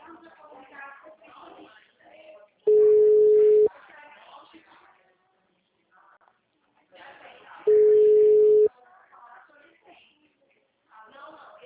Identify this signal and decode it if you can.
Ringback tone (Europe, 425 Hz), with speech.
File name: out- -2010- - - .84057